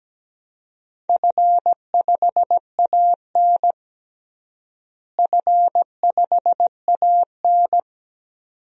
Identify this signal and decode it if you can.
Morse code, heard as a keyed tone, about 17 wpm, 700 Hz. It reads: F5AN F5AN
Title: DXCC-100 Call Signs - 4 Characters - Repeated 2x - 17wpm